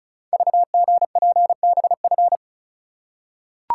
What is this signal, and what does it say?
Morse code, heard as a keyed tone, about 36 wpm, 700 Hz. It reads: VGPBF
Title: Code Groups - 5 Letters - 35wpm